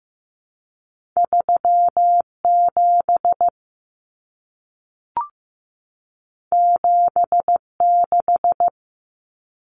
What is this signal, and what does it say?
Morse code, heard as a keyed tone, about 15 wpm, 700 Hz. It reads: 37 76
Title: Common Cut Numbers - 2-Digit 15wpm